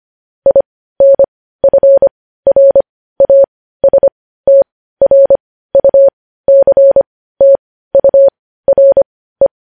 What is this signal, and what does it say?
Morse code, heard as a keyed tone, about 25 wpm, spaced as for 15 wpm, 550 Hz. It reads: INFRASTRUCTURE
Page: news-headlines-in-morse-code-at-15-wpm